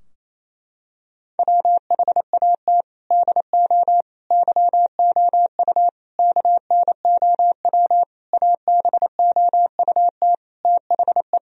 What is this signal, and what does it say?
Morse code, heard as a keyed tone, about 28 wpm, 700 Hz. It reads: WHAT DO YOU KNOW ABOUT THE
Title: Sentences from Top 1000 Words - Review Part 4 - 28wpm